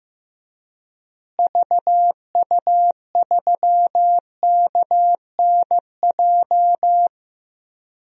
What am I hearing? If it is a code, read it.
Morse code, heard as a keyed tone, about 15 wpm, 700 Hz. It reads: VU3KNJ
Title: DXCC-100 Call Signs - 6 Characters - Repeated 2x - 15wpm